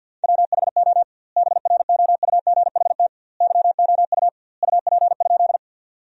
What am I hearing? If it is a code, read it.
Morse code, heard as a keyed tone, about 48 wpm, 700 Hz. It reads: WHO BROUGHT YOU UP?
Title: 2x Word-Spacing - Sentences from Top 500 Words - 50wpm